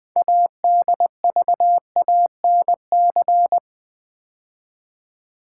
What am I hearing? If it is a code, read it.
Morse code, heard as a keyed tone, about 20 wpm, 700 Hz. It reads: ADVANC
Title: Build A Word - 2 Letters At A Time - 8-Letter Words - 20wpm